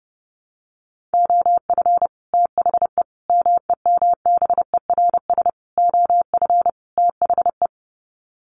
Morse code, heard as a keyed tone, about 30 wpm, 700 Hz. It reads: OF THE MEMBERS OF THE
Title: Sets of 5 Words Encore2 30wpm